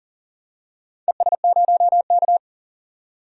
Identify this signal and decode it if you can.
Morse code, heard as a keyed tone, about 38 wpm, 700 Hz. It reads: ES0K
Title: DXCC-100 Call Signs - 4 Characters - 40wpm